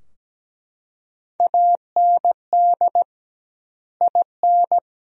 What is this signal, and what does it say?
Morse code, heard as a keyed tone, about 17 wpm, 700 Hz. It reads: AND IN
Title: 2x Word-Spacing - Sentences from Top 200 Words - 17wpm